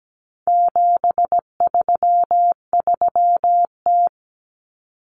Morse code, heard as a keyed tone, about 17 wpm, 700 Hz. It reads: 733T
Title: Common Cut Numbers - 4-Digit 17wpm